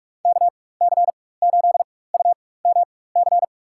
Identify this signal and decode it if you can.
Morse code, heard as a keyed tone, about 41 wpm, spaced as for 20 wpm, 700 Hz. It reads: KC8UMC
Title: USA Call Signs - 6 Characters - 4x Character Spacing - 45wpm